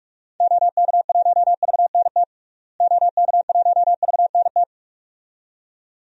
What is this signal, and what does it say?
Morse code, heard as a keyed tone, about 42 wpm, 700 Hz. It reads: OK1VNT OK1VNT
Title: DXCC-100 Call Signs - 6 Characters - Repeated 2x - 45wpm